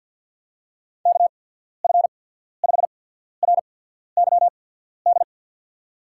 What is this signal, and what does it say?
Morse code, heard as a keyed tone, about 53 wpm, spaced as for 14 wpm, 700 Hz. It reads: KF5RYD
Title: USA Call Signs - 6 Characters - 8x Character Spacing - 50wpm